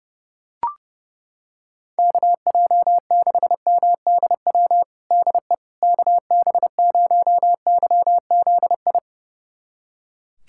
Morse code, heard as a keyed tone, about 30 wpm, 700 Hz. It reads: KJ6MDW DE KB0YZI
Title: USA Callsign DE Callsign 30wpm